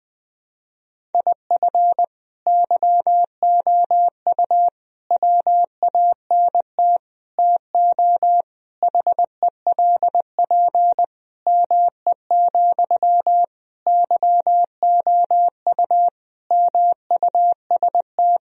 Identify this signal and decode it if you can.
Morse code, heard as a keyed tone, about 19 wpm, 700 Hz. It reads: IF YOU WANT TO HELP ME, YOU MUST
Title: Sentences from Top 1000 Words - Review Part 1 - 20wpm